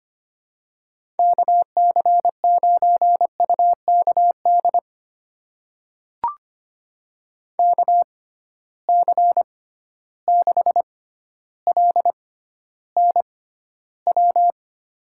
Morse code, heard as a keyed tone, about 25 wpm, spaced as for 13 wpm, 700 Hz. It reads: KC9UKD KC6LNW
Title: USA Call Signs - 6 Characters - 6x Character Spacing - 25wpm